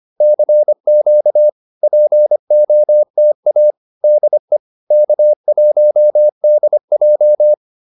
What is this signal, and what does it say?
Morse code, heard as a keyed tone, about 25 wpm, 600 Hz. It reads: CQ POTA DE K1DJ